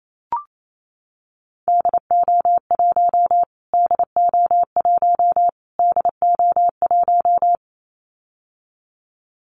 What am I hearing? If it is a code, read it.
Morse code, heard as a keyed tone, about 28 wpm, 700 Hz. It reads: DO1 DO1 DO1
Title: Callsign Prefixes for Top 25 DXCC - Repeated 3x - 28wpm